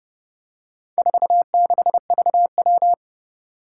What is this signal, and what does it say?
Morse code, heard as a keyed tone, about 30 wpm, 700 Hz. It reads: 46VW
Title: Four Challenging Letter-Number Combinations 30wpm